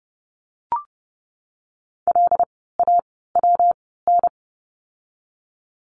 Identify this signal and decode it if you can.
Morse code, heard as a keyed tone, about 30 wpm, spaced as for 17 wpm, 700 Hz. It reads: LAWN
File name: course / 30:10wpm / 115 Lesson 19f - Letter w - Words 30wpm